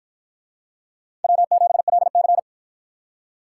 Morse code, heard as a keyed tone, about 53 wpm, 700 Hz. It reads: W7LC